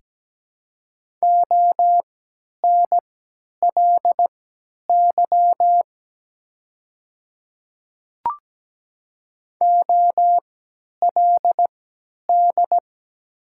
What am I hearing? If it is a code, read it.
Morse code, heard as a keyed tone, about 17 wpm, spaced as for 10 wpm, 700 Hz. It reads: ONLY OLD
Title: 3x Character Spacing - Top 200 Words - 17wpm